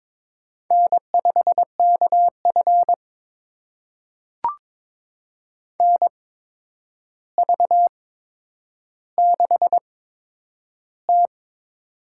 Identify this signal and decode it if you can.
Morse code, heard as a keyed tone, about 22 wpm, spaced as for 9 wpm, 700 Hz. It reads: N5KF NV6T
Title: USA Call Signs - 4 Characters - 8x Character Spacing - 22wpm